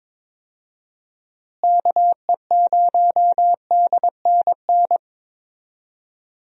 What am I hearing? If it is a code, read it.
Morse code, heard as a keyed tone, about 22 wpm, 700 Hz. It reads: KE0DNN
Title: USA Call Signs - 6 Characters - Repeated 2x - 22wpm